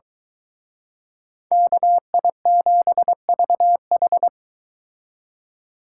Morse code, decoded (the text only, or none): KI7VH